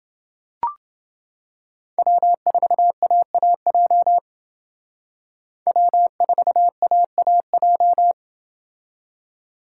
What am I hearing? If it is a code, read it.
Morse code, heard as a keyed tone, about 28 wpm, 700 Hz. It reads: W4AAJ W4AAJ